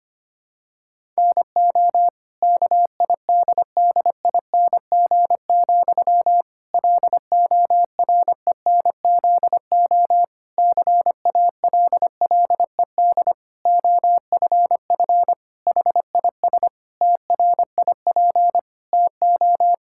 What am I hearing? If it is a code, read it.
Morse code, heard as a keyed tone, about 25 wpm, 700 Hz. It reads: NO KIDDING, LORENZO CALLED OFF HIS TRIP TO
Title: Pangram Challenge v2 - 25wpm